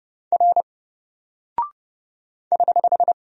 Morse code, heard as a keyed tone, about 30 wpm, 700 Hz. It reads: R <HH>
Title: Single Character (Letter:Number:Punctuation:Prosign) 30wpm